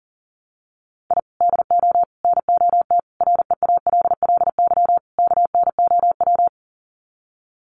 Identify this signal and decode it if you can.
Morse code, heard as a keyed tone, about 40 wpm, 700 Hz. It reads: I DO NOT REALLY KNOW
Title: Sets of 5 Words Encore 40wpm